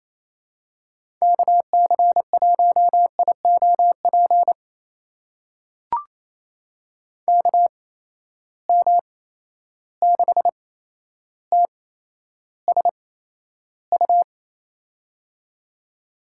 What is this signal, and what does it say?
Morse code, heard as a keyed tone, about 28 wpm, spaced as for 12 wpm, 700 Hz. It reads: KC1IOP KM6TSU
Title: USA Call Signs - 6 Characters - 8x Character Spacing - 28wpm